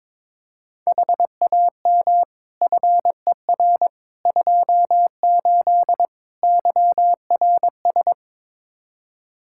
Morse code, heard as a keyed tone, about 22 wpm, 700 Hz. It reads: HAM FER 28 YRS
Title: Practice QSO - Exchange 3 - More Info - 22wpm